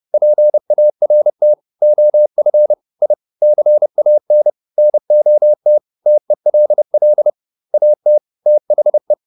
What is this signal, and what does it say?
Morse code, heard as a keyed tone, about 30 wpm, 600 Hz. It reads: PART OF I CAN NOT TELL AT THE